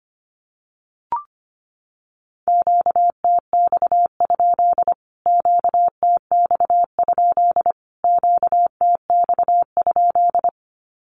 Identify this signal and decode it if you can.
Morse code, heard as a keyed tone, about 25 wpm, 700 Hz. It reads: QTX? QTX? QTX?
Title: Q-Codes Repeated 3x 25wpm